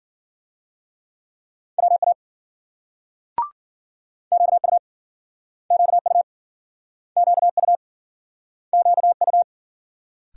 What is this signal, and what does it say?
Morse code, heard as a keyed tone, about 54 wpm, 700 Hz. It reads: WA QU QU QU QU